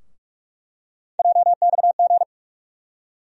Morse code, heard as a keyed tone, about 45 wpm, 700 Hz. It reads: JXG
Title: USA Call Sign Postfixes - 3 Letters 45wpm